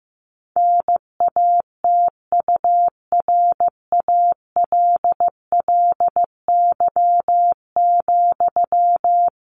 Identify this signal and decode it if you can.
Morse code, heard as a keyed tone, about 15 wpm, 700 Hz. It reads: NATURALLY,
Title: Sentences from Top 1500 Words 15wpm